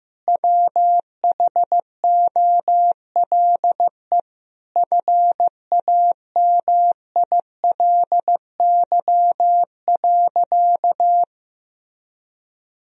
Morse code, heard as a keyed tone, about 15 wpm, 700 Hz. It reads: WHOLE FAMILY.
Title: Sentences from Top 1000 Words - Review Part 1 - 15wpm